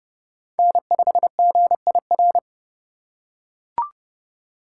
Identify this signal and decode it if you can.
Morse code, heard as a keyed tone, about 31 wpm, 700 Hz. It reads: N5GIR